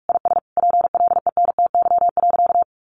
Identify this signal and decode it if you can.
Morse code, heard as a keyed tone, about 44 wpm, 700 Hz. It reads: IS PLENTY.
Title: Sentences from Top 1300 Words 45wpm